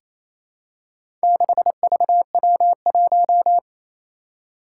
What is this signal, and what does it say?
Morse code, heard as a keyed tone, about 27 wpm, 700 Hz. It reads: 6VW1
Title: Four Challenging Letter-Number Combinations 28wpm